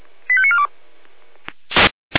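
Five-tone selective call (ZVEI), heard as quick runs of five tones, tones of about 70 ms. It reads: 85942
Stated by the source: Five-tone ZVEI selective call 85491, received by SDR.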